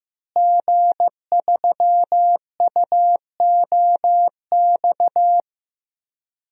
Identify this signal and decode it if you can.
Morse code, heard as a keyed tone, about 15 wpm, 700 Hz. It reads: G3UOX
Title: DXCC-100 Call Signs - 5 Characters - Repeated 2x - 15wpm